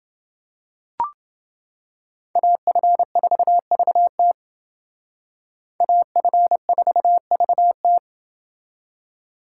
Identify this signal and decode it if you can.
Morse code, heard as a keyed tone, about 28 wpm, 700 Hz. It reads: AF4VT AF4VT